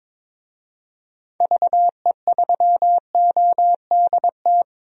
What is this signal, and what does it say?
Morse code, heard as a keyed tone, about 22 wpm, 700 Hz. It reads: VE3ODT
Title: DXCC-100 Call Signs - 6 Characters - Repeated 3x - 22wpm